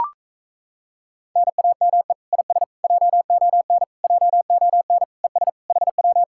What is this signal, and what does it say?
Morse code, heard as a keyed tone, about 38 wpm, 700 Hz. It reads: NAME IS JON JON ES HW